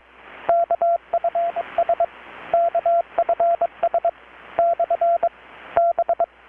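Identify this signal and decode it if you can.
Morse code, heard as a keyed tone, about 21 wpm, 680 Hz, with radio noise. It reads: KFS KFS / B
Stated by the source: Morse from KPH's Night of Nights XX maritime radiotelegraphy event, received in Chiapas, Mexico about 3,800 km away on a Kenwood R-600 with a grounded 10 m long-wire antenna.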